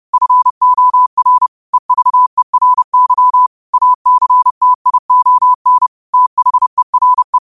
Morse code, heard as a keyed tone, about 30 wpm, 1000 Hz. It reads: FOR EVERY ACTION THERE
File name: qotdfiles / 30wpm